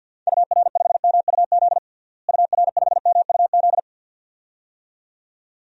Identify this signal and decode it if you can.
Morse code, heard as a keyed tone, about 46 wpm, 700 Hz. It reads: UR5MUZ UR5MUZ
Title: DXCC-100 Call Signs - 6 Characters - Repeated 2x - 50wpm